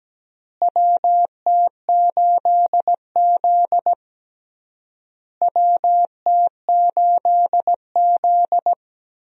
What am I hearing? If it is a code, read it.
Morse code, heard as a keyed tone, about 17 wpm, 700 Hz. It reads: WT8Z WT8Z
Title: USA Call Signs - 4 Characters - Repeated 3x - 17wpm